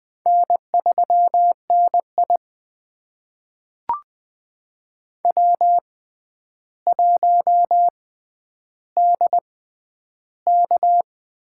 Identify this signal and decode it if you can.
Morse code, heard as a keyed tone, about 20 wpm, spaced as for 10 wpm, 700 Hz. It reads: N3NI W1DK
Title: USA Call Signs - 6x Character Spacing - 20wpm